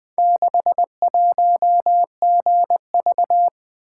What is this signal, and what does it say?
Morse code, heard as a keyed tone, about 20 wpm, 700 Hz. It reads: 61GV
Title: Four Letter-Number Combinations - 20wpm